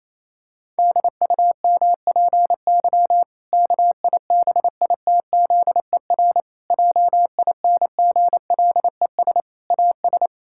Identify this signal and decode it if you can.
Morse code, heard as a keyed tone, about 28 wpm, 700 Hz. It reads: DUMPY KIBITZER JINGLES AS
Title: Pangram Challenge v2 - 28wpm